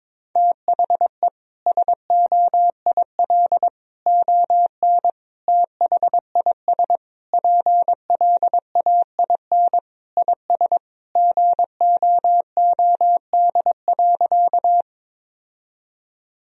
Morse code, heard as a keyed tone, about 22 wpm, 700 Hz. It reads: THE SOIL ON THIS PLAIN IS GOOD.
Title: Sentences from Top 1000 Words - Review Part 4 - 22wpm